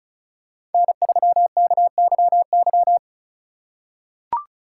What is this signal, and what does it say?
Morse code, heard as a keyed tone, about 36 wpm, 700 Hz. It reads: N3KYY
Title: USA Call Signs - 5 Characters - 2x Character Spacing - 35wpm